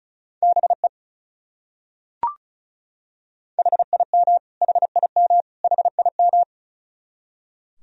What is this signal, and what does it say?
Morse code, heard as a keyed tone, about 34 wpm, 700 Hz. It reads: BE HIM HIM HIM
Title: Top 100 Words Repeated 35wpm